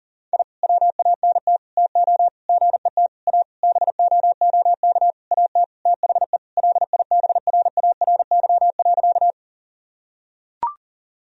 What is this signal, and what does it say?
Morse code, heard as a keyed tone, about 40 wpm, 700 Hz. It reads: I WANT TO GET A BOOK AT THE LIBRARY.
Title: Sentences from Top 1000 Words - Review Part 1 - 40wpm